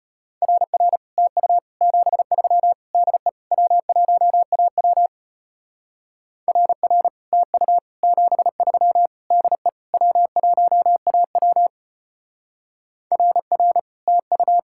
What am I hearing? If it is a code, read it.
Morse code, heard as a keyed tone, about 32 wpm, 700 Hz. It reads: RR TU 73 DE W1AW RR TU 73 DE W1AW RR TU